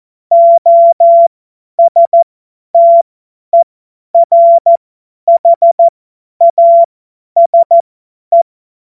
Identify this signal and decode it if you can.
Morse code, heard as a keyed tone, about 14 wpm, spaced as for 10 wpm, 680 Hz. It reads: OSTERHASE